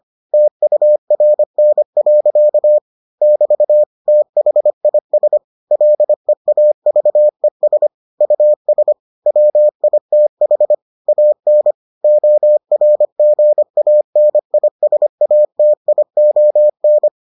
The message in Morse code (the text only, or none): TURN. <BT> THIS LEAVES US WITH AN ORGANISATION